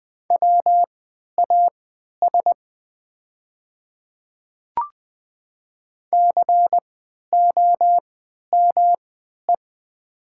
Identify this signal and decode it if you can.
Morse code, heard as a keyed tone, about 20 wpm, spaced as for 11 wpm, 700 Hz. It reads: WAS COME
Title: 3x Character Spacing - Top 100 Words - 20wpm